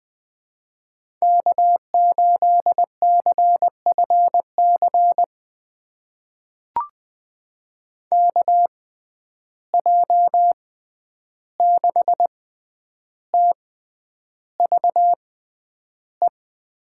Morse code, heard as a keyed tone, about 20 wpm, spaced as for 10 wpm, 700 Hz. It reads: K8CFC KJ6TVE
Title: USA Call Signs - 6x Character Spacing - 20wpm